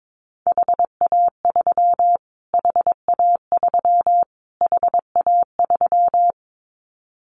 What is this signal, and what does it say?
Morse code, heard as a keyed tone, about 22 wpm, 700 Hz. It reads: HA3 HA3 HA3
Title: Callsign Prefixes for Top 25 DXCC - Repeated 3x - 22wpm